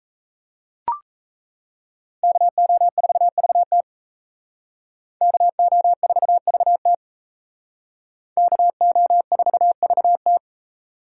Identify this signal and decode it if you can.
Morse code, heard as a keyed tone, about 35 wpm, 700 Hz. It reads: KO4VT KO4VT KO4VT
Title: Speed Racing - USA 2x2 Call Signs 28wpm